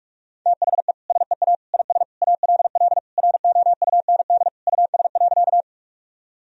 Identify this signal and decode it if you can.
Morse code, heard as a keyed tone, about 46 wpm, 700 Hz. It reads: THE SEA IS ALL ROUND US.
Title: Sentences from Top 1000 Words - Review Part 1 - 45wpm